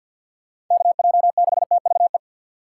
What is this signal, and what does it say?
Morse code, heard as a keyed tone, about 53 wpm, 700 Hz. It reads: KJ6TVE
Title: USA Call Signs - 6x Character Spacing - 50wpm